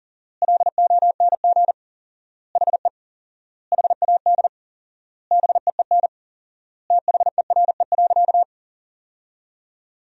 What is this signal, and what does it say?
Morse code, heard as a keyed tone, about 40 wpm, 700 Hz. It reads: LONG HE HAD BEEN THERE.
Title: 4x Word-Spacing - Sentences from Top 300 Words - 40wpm